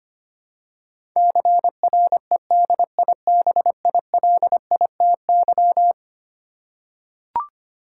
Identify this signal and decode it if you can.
Morse code, heard as a keyed tone, about 25 wpm, 700 Hz. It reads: CREDIBILITY
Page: morse-code-ninja-practice-25wpm